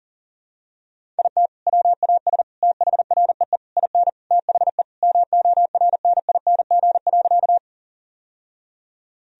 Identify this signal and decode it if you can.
Morse code, heard as a keyed tone, about 40 wpm, 700 Hz. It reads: IT WAS THREE IN THE MORNING.
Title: Sentences from Top 1000 Words - Review Part 1 - 40wpm